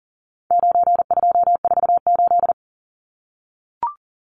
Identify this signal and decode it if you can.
Morse code, heard as a keyed tone, about 40 wpm, 700 Hz. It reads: 9248
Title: Numbers - 4-Digit 40wpm